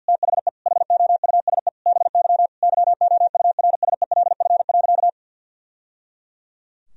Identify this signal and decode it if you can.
Morse code, heard as a keyed tone, about 46 wpm, 700 Hz. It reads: THE HOUSE BY YOURSELF.
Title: Sentences from Top 1000 Words - Review Part 4 - 50wpm